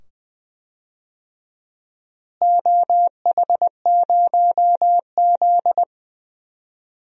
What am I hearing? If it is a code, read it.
Morse code, heard as a keyed tone, about 21 wpm, 700 Hz. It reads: OH0Z
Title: DXCC-100 Call Signs - 4 Characters - Repeated 2x - 20wpm